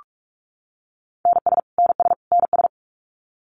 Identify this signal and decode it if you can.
Morse code, heard as a keyed tone, about 44 wpm, 700 Hz. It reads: NS NS NS
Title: Canadian Provinces and Territories - Repeated 3x - 45wpm